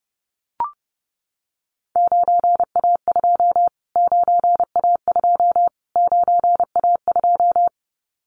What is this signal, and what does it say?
Morse code, heard as a keyed tone, about 30 wpm, 700 Hz. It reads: 9A2 9A2 9A2
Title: Callsign Prefixes for Top 25 DXCC - Repeated 3x - 30wpm